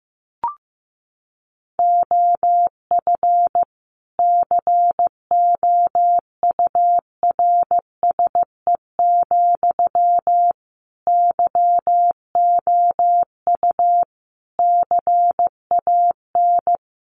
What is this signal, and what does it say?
Morse code, heard as a keyed tone, about 15 wpm, 700 Hz. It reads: OF COURSE, YOU CAN
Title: Sentences from Top 1500 Words 15wpm